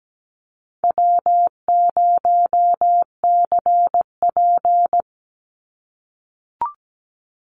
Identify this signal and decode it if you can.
Morse code, heard as a keyed tone, about 17 wpm, 700 Hz. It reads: W0CP
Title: NA SOTA Chaser Callsigns 17wpm